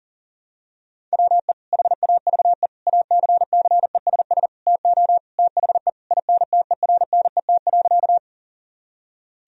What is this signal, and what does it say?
Morse code, heard as a keyed tone, about 40 wpm, 700 Hz. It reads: WE HAVE ACCESS TO THE INTERNET.
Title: Sentences from Top 1000 Words - Review Part 4 - 40wpm